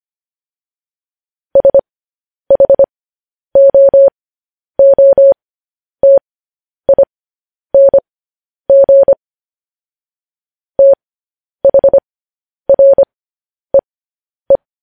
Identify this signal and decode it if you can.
Morse code, heard as a keyed tone, about 25 wpm, spaced as for 10 wpm, 550 Hz. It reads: SHOOTING THREE